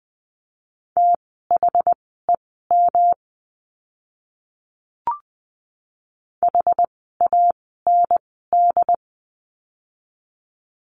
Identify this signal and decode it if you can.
Morse code, heard as a keyed tone, about 20 wpm, spaced as for 14 wpm, 700 Hz. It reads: THEM HAND
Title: 244 Lesson 39f - Letter z - Words 20wpm